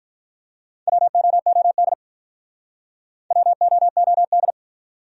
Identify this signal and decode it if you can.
Morse code, heard as a keyed tone, about 47 wpm, 700 Hz. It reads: WOOD WOOD